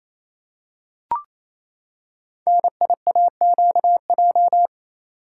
Morse code, heard as a keyed tone, about 28 wpm, 700 Hz. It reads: NIAQJ
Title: Code Groups - 5 Letters - 28wpm